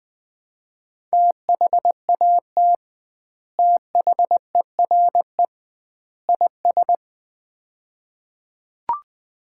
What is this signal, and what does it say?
Morse code, heard as a keyed tone, about 20 wpm, 700 Hz. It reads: THAT THERE IS
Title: 2x Word-Spacing - Sets of 3 Words - 20wpm